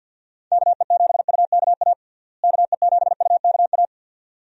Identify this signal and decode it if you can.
Morse code, heard as a keyed tone, about 46 wpm, 700 Hz. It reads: KE7UKA KE7UKA
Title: USA Call Signs - 6 Characters - Repeated 2x - 50wpm